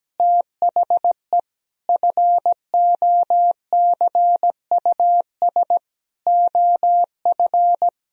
Morse code, heard as a keyed tone, about 17 wpm, 700 Hz. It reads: THE FOCUS OF
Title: Sentences from Top 1000 Words - Review Part 4 - 17wpm